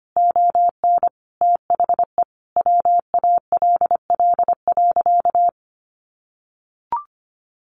Morse code, heard as a keyed tone, about 25 wpm, 700 Hz. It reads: ON THE WALL.
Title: Sentences from Top 1100 Words 25wpm